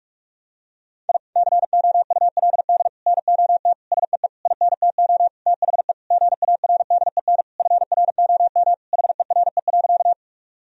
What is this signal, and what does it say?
Morse code, heard as a keyed tone, about 41 wpm, 700 Hz. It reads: I COULD NOT SEE INTO THE GARDEN FROM HERE.
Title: Sentences from Top 1000 Words - Review Part 4 - 45wpm